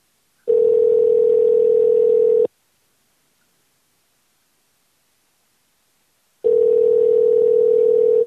Ringback tone (North America), 440 + 480 Hz.